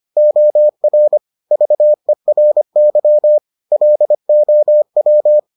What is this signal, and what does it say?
Morse code, heard as a keyed tone, about 25 wpm, 600 Hz. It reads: OR VERY LOW